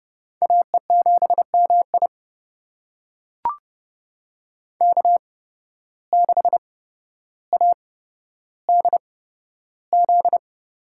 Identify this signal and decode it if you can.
Morse code, heard as a keyed tone, about 30 wpm, spaced as for 13 wpm, 700 Hz. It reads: AE7MI K6ADZ
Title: USA Call Signs - 5 Characters - 8x Character Spacing - 30wpm